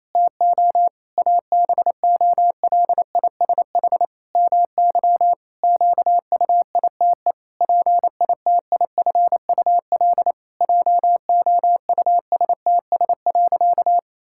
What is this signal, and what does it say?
Morse code, heard as a keyed tone, about 28 wpm, 700 Hz. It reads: TO ABOLISH MY QUITE PITIFUL JOUSTS.
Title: Pangram Challenge v2 - 28wpm